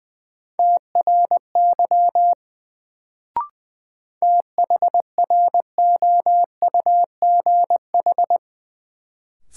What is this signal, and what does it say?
Morse code, heard as a keyed tone, about 20 wpm, 700 Hz. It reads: TRY THROUGH
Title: Sets of 2 Words - New Word Drill - 20wpm